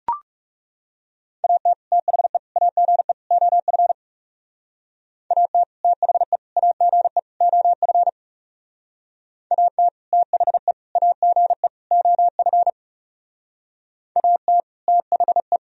Morse code, heard as a keyed tone, about 36 wpm, 700 Hz. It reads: AT THE AGE OF AT THE AGE OF AT THE AGE OF AT THE
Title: Speed-Racing - Sets of 4 Words 30wpm